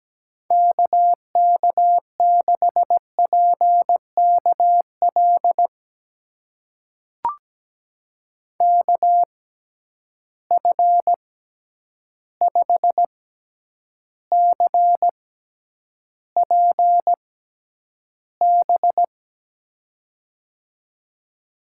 Morse code, heard as a keyed tone, about 17 wpm, spaced as for 9 wpm, 700 Hz. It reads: KK6PKL KF5CPB
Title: USA Call Signs - 6 Characters - 6x Character Spacing - 17wpm